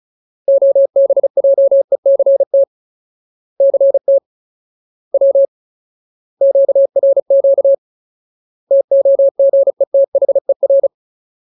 Morse code, heard as a keyed tone, about 34 wpm, 550 Hz. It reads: OBJECT CT W QRQ TOGETHER
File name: Ultimate Word List / 35WPM Ultimate